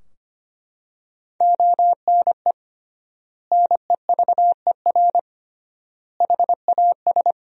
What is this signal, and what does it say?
Morse code, heard as a keyed tone, about 25 wpm, 700 Hz. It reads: ONE NEVER HAS
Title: 3x Word-Spacing - Sentences from Top 300 Words - 25wpm